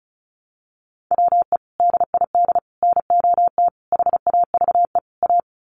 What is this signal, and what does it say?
Morse code, heard as a keyed tone, about 34 wpm, 700 Hz. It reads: WE DID NOT HAVE A
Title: Sets of 5 Words Encore2 35wpm